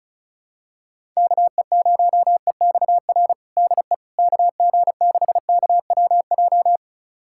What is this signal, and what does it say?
Morse code, heard as a keyed tone, about 36 wpm, 700 Hz. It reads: KE0EXR DE KG6KWJ
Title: USA Callsign DE Callsign - 6 Characters - 35wpm